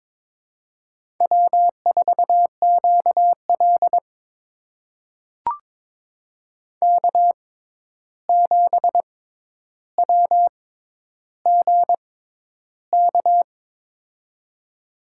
Morse code, heard as a keyed tone, about 22 wpm, spaced as for 11 wpm, 700 Hz. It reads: W4QL K7WGK
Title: USA Call Signs - 6x Character Spacing - 22wpm